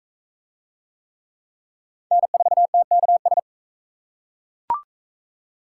Morse code, heard as a keyed tone, about 37 wpm, 700 Hz. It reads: N4TKS